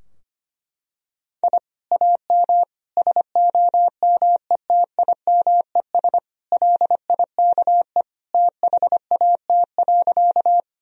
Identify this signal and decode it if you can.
Morse code, heard as a keyed tone, about 25 wpm, 700 Hz. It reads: I AM SOMETIMES LIKE THAT.
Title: Sentences from Top 1000 Words - Review Part 3 - 25wpm